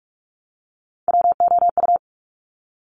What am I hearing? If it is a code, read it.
Morse code, heard as a keyed tone, about 46 wpm, 690 Hz. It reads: WOU
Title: Top Three Letter Combinations in Words 45wpm